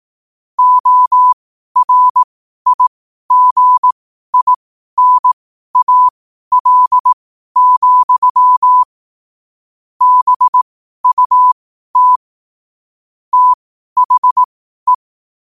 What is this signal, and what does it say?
Morse code, heard as a keyed tone, about 18 wpm, spaced as for 12 wpm, 1000 Hz. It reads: ORIGINAL, BUT THE
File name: qotdfiles / 12wpm